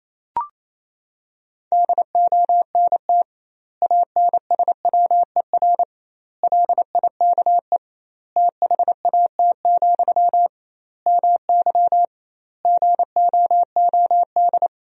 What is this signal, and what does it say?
Morse code, heard as a keyed tone, about 28 wpm, 700 Hz. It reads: DONT ANSWER LIKE THAT, MY GOOD
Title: 2x Word-Spacing - Sentences from Top 400 Words - 28wpm